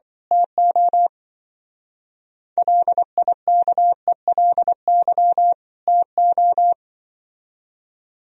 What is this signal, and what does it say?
Morse code, heard as a keyed tone, about 24 wpm, 700 Hz. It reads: TO LIKELY TO
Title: Speed-Racing - Sets of 2 Words 20wpm